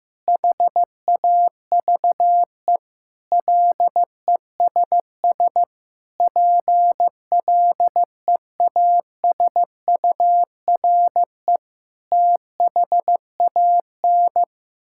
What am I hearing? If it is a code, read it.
Morse code, heard as a keyed tone, about 14 wpm, 700 Hz. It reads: HAVE LESS PLEASURE THAN
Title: Sentences from Top 1600 Words 15wpm